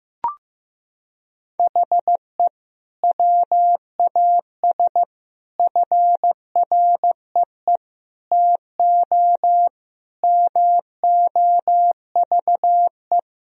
Morse code, heard as a keyed tone, about 15 wpm, 700 Hz. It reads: HE WAS FREE TO MOVE
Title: Sentences from Top 1000 Words - Review Part 1 - 15wpm